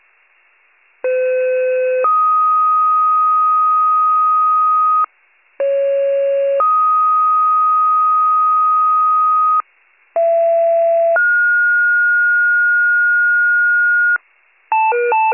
Two-tone page, heard as a one-second tone, then a three-second tone, about 510 then 1250 Hz, about 550 then 1250 Hz, about 670 then 1430 Hz. It is heard from 1.0 to 14.2 s.